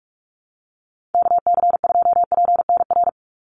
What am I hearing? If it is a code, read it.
Morse code, heard as a keyed tone, about 44 wpm, 700 Hz. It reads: KC2PNR
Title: USA Call Signs 45wpm